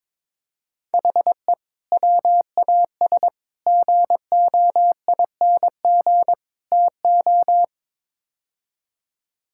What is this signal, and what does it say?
Morse code, heard as a keyed tone, about 22 wpm, 700 Hz. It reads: HE WAS GOING TO
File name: practice / 22wpm / Build A Phrase - 5-Word Phrases - 22wpm